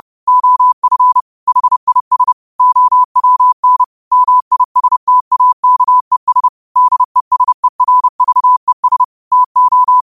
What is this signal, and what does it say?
Morse code, heard as a keyed tone, about 30 wpm, 1000 Hz. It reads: OF HIS OWN MISTAKES DESERVES TO